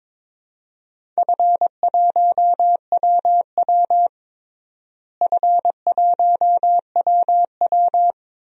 Morse code, heard as a keyed tone, about 21 wpm, 700 Hz. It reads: F1WW F1WW
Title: DXCC-100 Call Signs - 4 Characters - Repeated 3x - 22wpm